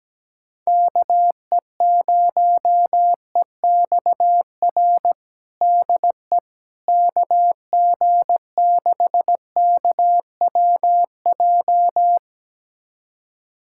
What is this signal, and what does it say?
Morse code, heard as a keyed tone, about 17 wpm, 700 Hz. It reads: KE0EXR DE KG6KWJ
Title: USA Callsign DE Callsign - 6 Characters - 17wpm